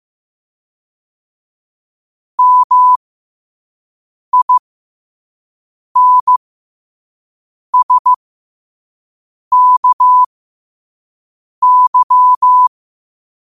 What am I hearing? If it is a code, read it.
Morse code, heard as a keyed tone, about 15 wpm, spaced as for 5 wpm, 1000 Hz. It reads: MINSKY